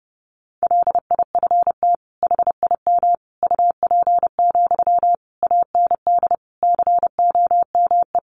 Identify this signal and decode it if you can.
Morse code, heard as a keyed tone, about 30 wpm, 700 Hz. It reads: LIFT HIM UP, AND COME
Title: Sentences from Top 1000 Words 30wpm